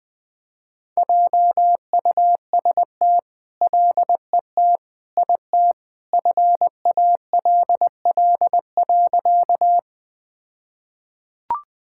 Morse code, heard as a keyed tone, about 20 wpm, 700 Hz. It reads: JUST LET IT FALL.
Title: All-Age Appropriate Jokes - 20wpm